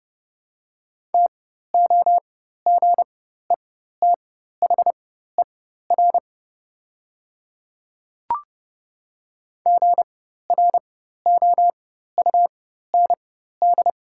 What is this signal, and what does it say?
Morse code, heard as a keyed tone, about 30 wpm, spaced as for 14 wpm, 700 Hz. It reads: TOGETHER GROUND
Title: 4x Character Spacing - Top 500 Words - 30wpm